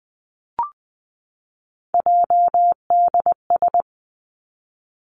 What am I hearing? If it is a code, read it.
Morse code, heard as a keyed tone, about 20 wpm, 700 Hz. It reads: JDS
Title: Three Challenging Letter-Number Combinations 20wpm